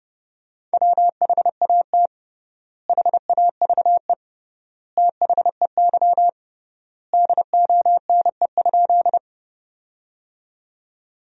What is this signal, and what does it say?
Morse code, heard as a keyed tone, about 32 wpm, 700 Hz. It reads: WHAT HAVE THEY DONE?
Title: 3x Word-Spacing - Sentences from Top 500 Words - 30wpm